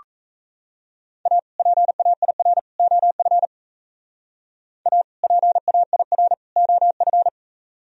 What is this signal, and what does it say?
Morse code, heard as a keyed tone, about 34 wpm, 700 Hz. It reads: A PAIR OF A PAIR OF
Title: Speed-Racing - Sets of 3 Words 28wpm